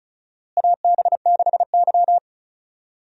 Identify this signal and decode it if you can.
Morse code, heard as a keyed tone, about 34 wpm, 700 Hz. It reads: AB6Y
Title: USA Call Signs - 4 Characters - Repeated 2x - 35wpm